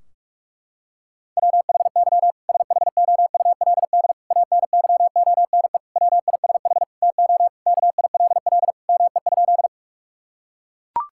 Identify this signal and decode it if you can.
Morse code, heard as a keyed tone, about 43 wpm, 700 Hz. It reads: WHY SHOULD ANYONE WISH TO KILL ME?
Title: Sentences from Top 1600 Words 45wpm